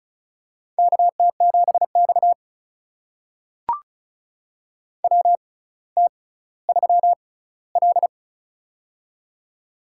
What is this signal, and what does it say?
Morse code, heard as a keyed tone, about 34 wpm, spaced as for 18 wpm, 700 Hz. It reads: KT7X WT3L
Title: USA Call Signs - 4 Characters - 6x Character Spacing - 35wpm